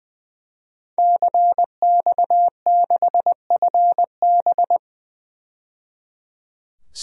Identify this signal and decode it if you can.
Morse code, heard as a keyed tone, about 20 wpm, 700 Hz. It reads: CX6FB
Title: DXCC-100 Call Signs - 5 Characters - 20wpm